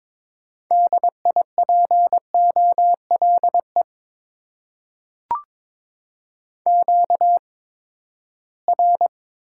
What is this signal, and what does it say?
Morse code, heard as a keyed tone, about 22 wpm, spaced as for 15 wpm, 700 Hz. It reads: DIPOLE QR
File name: Top 90 QSO Elements - 8x Character Spacing - 22wpm